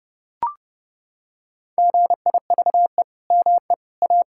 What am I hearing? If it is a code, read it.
Morse code, heard as a keyed tone, about 31 wpm, 700 Hz. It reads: GIVE ME A
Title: Sentences from Top 1600 Words 30wpm